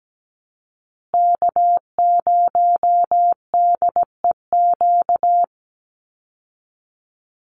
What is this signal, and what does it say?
Morse code, heard as a keyed tone, about 17 wpm, 700 Hz. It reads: K0DEQ
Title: Call Signs Repeated 2x - 17wpm